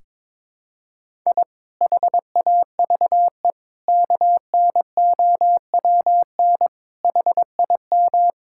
Morse code, heard as a keyed tone, about 22 wpm, 700 Hz. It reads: I HAVE KNOWN HIM
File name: practice / 22wpm / Sentences with 4-Word Phrases - 22wpm